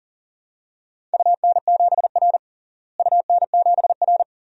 Morse code, heard as a keyed tone, about 39 wpm, 700 Hz. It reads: UN7R UN7R
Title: DXCC-100 Call Signs - 4 Characters - Repeated 2x - 40wpm